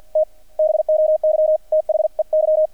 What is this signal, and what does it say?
Morse code, heard as a keyed tone, about 50 wpm, 620 Hz, with radio noise. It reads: T BOY THEY
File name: archived / CRSnet